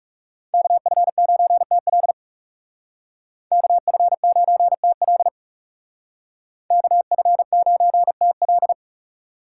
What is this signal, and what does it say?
Morse code, heard as a keyed tone, about 35 wpm, 700 Hz. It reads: KF9TL KF9TL KF9TL
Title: Speed Racing - USA 2x2 Call Signs 30wpm